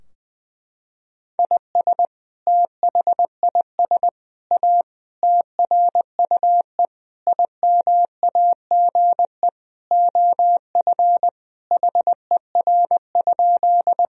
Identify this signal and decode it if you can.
Morse code, heard as a keyed tone, about 21 wpm, 700 Hz. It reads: IS THIS A TRUE IMAGE OF HER?
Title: Sentences from Top 1000 Words - Review Part 4 - 20wpm